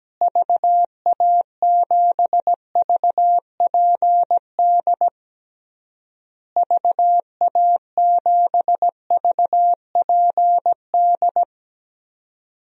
Morse code, heard as a keyed tone, about 17 wpm, 700 Hz. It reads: VA7VPD VA7VPD
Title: DXCC-100 Call Signs - 6 Characters - Repeated 3x - 17wpm